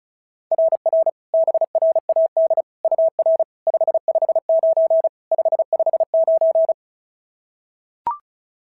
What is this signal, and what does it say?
Morse code, heard as a keyed tone, about 36 wpm, 690 Hz. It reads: RR BRAD UR 559 559
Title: SOTA QSO Practice 35wpm